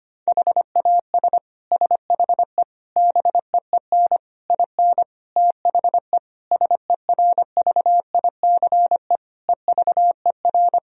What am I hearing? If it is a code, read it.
Morse code, heard as a keyed tone, about 25 wpm, 700 Hz. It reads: HAS SHE BEEN IN THE SERVICE EVER
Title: Sentences from Top 1000 Words - Review Part 4 - 25wpm